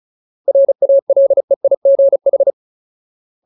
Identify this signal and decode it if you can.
Morse code, heard as a keyed tone, about 34 wpm, 550 Hz. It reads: RALEIGH
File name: Ultimate Word List / 35WPM Ultimate